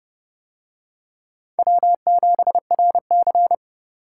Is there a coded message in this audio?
Yes: Morse code from 1.6 to 3.5 s, heard as a keyed tone, about 27 wpm, 700 Hz. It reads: W7RC